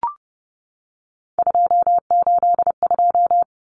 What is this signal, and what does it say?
Morse code, heard as a keyed tone, about 30 wpm, 700 Hz. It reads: 282
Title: Numbers - 3-Digit 30wpm